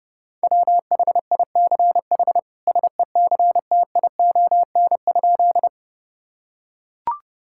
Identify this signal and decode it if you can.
Morse code, heard as a keyed tone, about 30 wpm, 700 Hz. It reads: WHICH SECTION?
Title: Sentences from Top 1000 Words - Review Part 5 - 30wpm